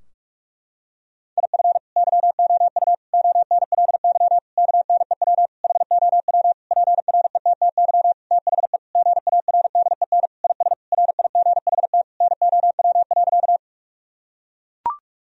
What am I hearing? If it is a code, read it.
Morse code, heard as a keyed tone, about 41 wpm, 700 Hz. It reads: IF YOU ONLY KNEW HOW PRETTY THE GARDEN IS RIGHT NOW.
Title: Sentences from Top 1000 Words - Review Part 4 - 45wpm